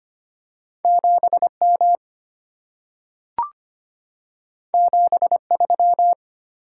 Morse code, heard as a keyed tone, about 25 wpm, 690 Hz. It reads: 7M 73
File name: Two Letter-Number Combinations - 25wpm